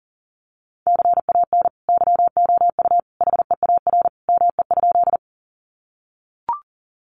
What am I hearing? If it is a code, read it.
Morse code, heard as a keyed tone, about 40 wpm, 700 Hz. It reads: CAN YOU HEAR ME?
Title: Sentences from Top 300 Words 40wpm